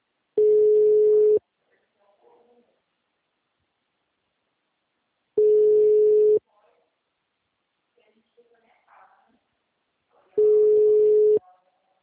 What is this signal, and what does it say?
Ringback tone (Europe, 425 Hz), with speech.
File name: out- -2016- - - .13